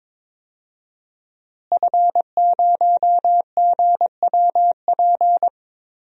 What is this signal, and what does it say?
Morse code, heard as a keyed tone, about 22 wpm, 700 Hz. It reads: F0GWP
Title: DXCC-100 Call Signs - 5 Characters - Repeated 2x - 22wpm